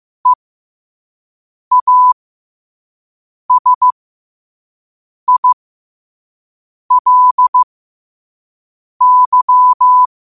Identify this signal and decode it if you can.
Morse code, heard as a keyed tone, about 15 wpm, spaced as for 5 wpm, 1000 Hz. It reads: EASILY
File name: qotdfiles / 5wpm